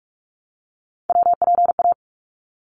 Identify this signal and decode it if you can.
Morse code, heard as a keyed tone, about 44 wpm, 700 Hz. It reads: WPA
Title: ARRL Sections 45wpm